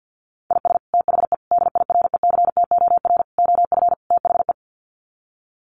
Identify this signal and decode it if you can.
Morse code, heard as a keyed tone, about 50 wpm, 700 Hz. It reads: IS THE DIRECTOR OF THE
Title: Sets of 5 Words Encore2 50wpm